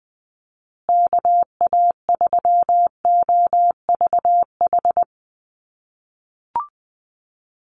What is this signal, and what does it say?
Morse code, heard as a keyed tone, about 20 wpm, 700 Hz. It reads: KA3OVH
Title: USA Call Signs Repeated 2x 20wpm